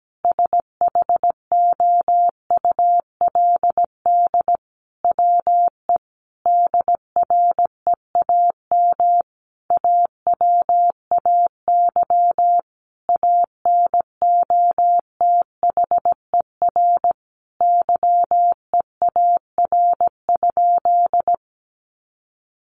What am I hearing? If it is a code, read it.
Morse code, heard as a keyed tone, about 17 wpm, 700 Hz. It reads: SHOULD WE DREAM AWAY ANOTHER YEAR?
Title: Sentences from Top 1500 Words 17wpm